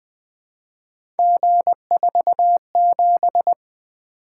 Morse code, heard as a keyed tone, about 20 wpm, 700 Hz. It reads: G47
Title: Three Letter-Number Combinations - 20wpm